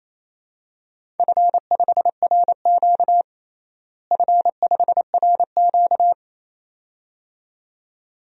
Morse code, heard as a keyed tone, about 28 wpm, 700 Hz. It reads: F5RQ F5RQ
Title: DXCC-100 Call Signs - 4 Characters - Repeated 2x - 28wpm